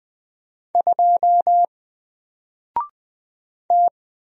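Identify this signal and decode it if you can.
Morse code, heard as a keyed tone, about 20 wpm, 700 Hz. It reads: 2 T
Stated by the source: Single letter and number Morse drill at 20 wpm, from N4LCD.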